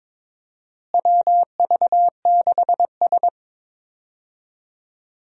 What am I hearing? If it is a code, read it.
Morse code, heard as a keyed tone, about 22 wpm, 700 Hz. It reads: WV6S
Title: USA Call Signs - 4 Characters - Repeated 3x - 22wpm